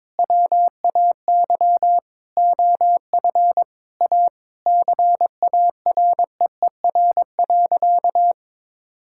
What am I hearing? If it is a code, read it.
Morse code, heard as a keyed tone, about 21 wpm, 700 Hz. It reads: WAY OF A CAREER.
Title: Sentences from Top 1000 Words - Review Part 6 - 22wpm